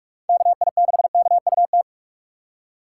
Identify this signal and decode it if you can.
Morse code, heard as a keyed tone, about 44 wpm, 700 Hz. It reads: KI6KUT
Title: USA Call Signs - 6 Characters - 6x Character Spacing - 45wpm